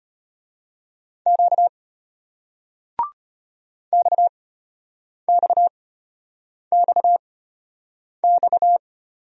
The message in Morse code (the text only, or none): Q XXXX